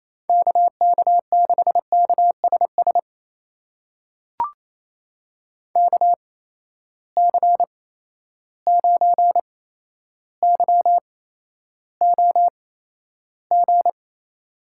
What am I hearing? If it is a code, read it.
Morse code, heard as a keyed tone, about 28 wpm, spaced as for 12 wpm, 700 Hz. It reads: KK6KSS KC9YOG
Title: USA Call Signs - 6 Characters - 8x Character Spacing - 28wpm